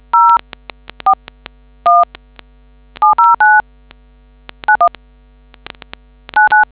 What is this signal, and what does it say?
DTMF, heard as telephone touch tones, tones of about 180 ms. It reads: 041709919